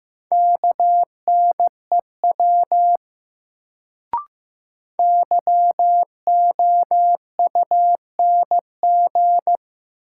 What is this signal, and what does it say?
Morse code, heard as a keyed tone, about 15 wpm, 700 Hz. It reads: KNEW YOUNG
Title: Sentences from Top 400 Words - New Word Drill - 15wpm